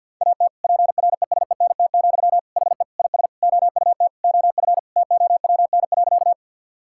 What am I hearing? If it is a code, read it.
Morse code, heard as a keyed tone, about 51 wpm, 700 Hz. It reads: AT PRESENT, HE IS OUT OF TOWN.
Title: Sentences from Top 1000 Words - Review Part 5 - 50wpm